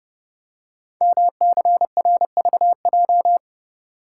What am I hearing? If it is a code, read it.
Morse code, heard as a keyed tone, about 30 wpm, 700 Hz. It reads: MCRVJ
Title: Code Groups - 5 Letters - 30wpm